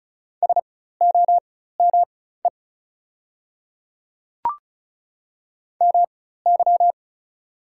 Morse code, heard as a keyed tone, about 36 wpm, spaced as for 16 wpm, 700 Hz. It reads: SOME MY
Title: 4x Character Spacing - Top 100 Words - 35wpm